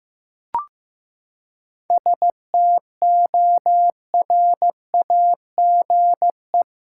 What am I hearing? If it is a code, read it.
Morse code, heard as a keyed tone, about 15 wpm, 700 Hz. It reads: STORAGE
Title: Seven Letter Words - 15wpm